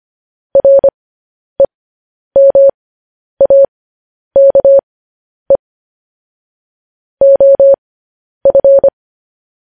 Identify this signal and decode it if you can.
Morse code, heard as a keyed tone, about 25 wpm, spaced as for 10 wpm, 550 Hz. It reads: REMAKE OF
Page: news-headlines-in-morse-code-at-10-wpm